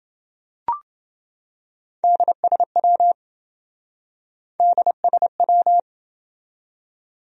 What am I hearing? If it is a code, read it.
Morse code, heard as a keyed tone, about 27 wpm, 700 Hz. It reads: DSW DSW